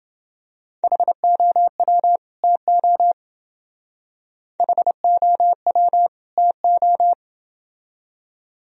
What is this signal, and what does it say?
Morse code, heard as a keyed tone, about 28 wpm, 700 Hz. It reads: HOW TO HOW TO